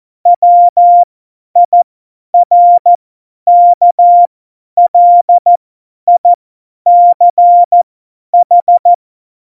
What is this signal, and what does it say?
Morse code, heard as a keyed tone, about 14 wpm, spaced as for 10 wpm, 700 Hz. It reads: WIRKLICH
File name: Morsekurs / Witze / Witz 26. Juni 2018, 50 F70